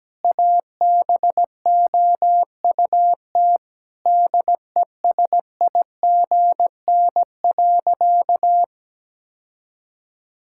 Morse code, heard as a keyed tone, about 17 wpm, 700 Hz. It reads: ABOUT DESIGN.
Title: Sentences from Top 1000 Words - Review Part 5 - 17wpm